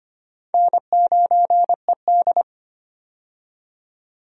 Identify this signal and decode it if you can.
Morse code, heard as a keyed tone, about 25 wpm, 700 Hz. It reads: N9ED
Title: USA Call Signs - 4 Characters - 25wpm